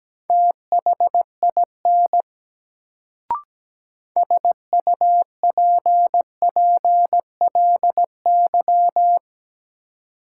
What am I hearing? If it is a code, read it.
Morse code, heard as a keyed tone, about 17 wpm, 700 Hz. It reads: THIN SUPPLY
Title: Sentences from Top 1200 Words - New Word Drill Alt - 17wpm